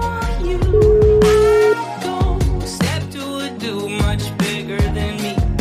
Ringback tone (Europe, 425 Hz), with noise.